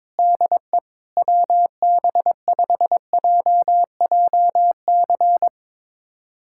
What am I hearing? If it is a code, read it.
Morse code, heard as a keyed tone, about 22 wpm, 700 Hz. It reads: DE WB5JJC
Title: USA Callsign DE Callsign 22wpm